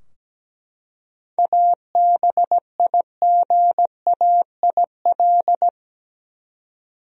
Morse code, heard as a keyed tone, about 17 wpm, 700 Hz. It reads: ABIGAIL
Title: USA Common Names - 17wpm